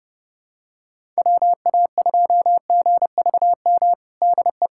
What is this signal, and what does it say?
Morse code, heard as a keyed tone, about 32 wpm, 700 Hz. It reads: WA2GVM DE
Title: USA Callsign DE Callsign - 6 Characters - 30wpm